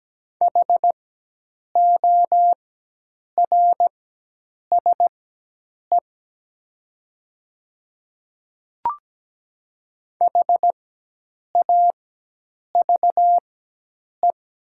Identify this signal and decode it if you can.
Morse code, heard as a keyed tone, about 17 wpm, spaced as for 8 wpm, 700 Hz. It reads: HORSE HAVE
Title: 4x Character Spacing - Top 500 Words - 17wpm